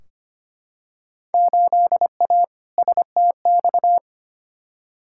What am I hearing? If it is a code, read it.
Morse code, heard as a keyed tone, about 25 wpm, 700 Hz. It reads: 8A STX
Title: Field Day Exchanges Encore - 25wpm